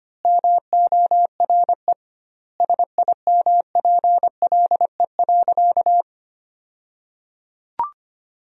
Morse code, heard as a keyed tone, about 25 wpm, 700 Hz. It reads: MORE SIMPLE.
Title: 2x Word-Spacing - Sentences from Top 400 Words - 25wpm